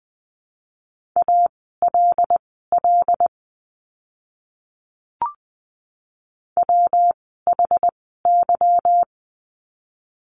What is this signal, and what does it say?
Morse code, heard as a keyed tone, about 20 wpm, spaced as for 15 wpm, 700 Hz. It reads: ALL WHY
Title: 146 Lesson 24f - Letter y - Words 20wpm